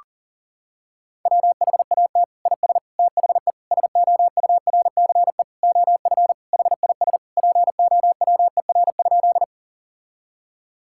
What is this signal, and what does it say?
Morse code, heard as a keyed tone, about 40 wpm, 700 Hz. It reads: WHAT IS THE SOURCE OF HIS POWER?
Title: Sentences from Top 1000 Words - Review Part 5 - 40wpm